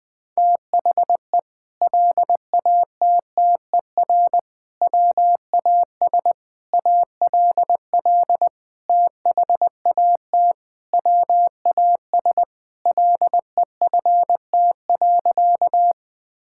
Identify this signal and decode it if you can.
Morse code, heard as a keyed tone, about 20 wpm, 700 Hz. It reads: THE LATTER WAS ALL THAT WAS LEFT.
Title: Sentences from Top 1600 Words 20wpm